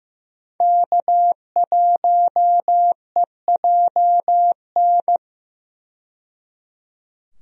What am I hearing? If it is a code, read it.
Morse code, heard as a keyed tone, about 16 wpm, 700 Hz. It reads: K1EJN
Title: USA Call Signs - 5 Characters - 15wpm